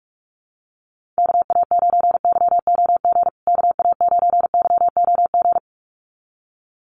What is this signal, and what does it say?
Morse code, heard as a keyed tone, about 44 wpm, 700 Hz. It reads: KA9YOG KA9YOG
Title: USA Call Signs Repeated 2x 45wpm